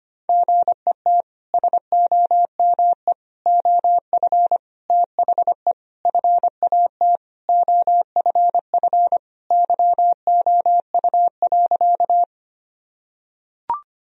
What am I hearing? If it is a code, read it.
Morse code, heard as a keyed tone, about 25 wpm, 700 Hz. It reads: GET SOME OF THE FAT OFF YOU.
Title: Sentences from Top 1000 Words - Review Part 4 - 25wpm